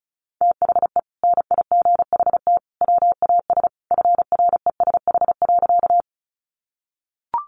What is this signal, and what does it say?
Morse code, heard as a keyed tone, about 34 wpm, 700 Hz. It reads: THE NIGHT WAS FRESH.
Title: Sentences from Top 1200 Words 35wpm